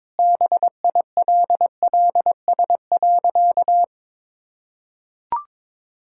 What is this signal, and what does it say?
Morse code, heard as a keyed tone, about 22 wpm, 690 Hz. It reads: BILLS.
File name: practice / 22wpm / Easy Sentences with Hints - 22wpm